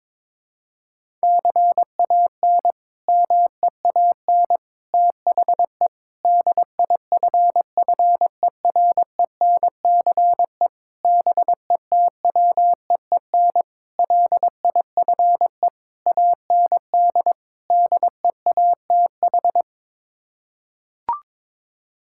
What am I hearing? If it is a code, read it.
Morse code, heard as a keyed tone, about 22 wpm, 700 Hz. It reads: CAN MEAN THE DIFFERENCE BETWEEN LIFE AND DEATH
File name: practice / 22wpm / Sets of 8 Words - 22wpm